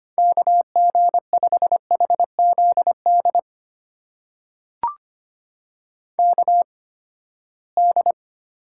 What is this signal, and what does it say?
Morse code, heard as a keyed tone, about 25 wpm, spaced as for 17 wpm, 700 Hz. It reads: KG5HZD KD
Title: USA Call Signs - 6 Characters - 8x Character Spacing - 25wpm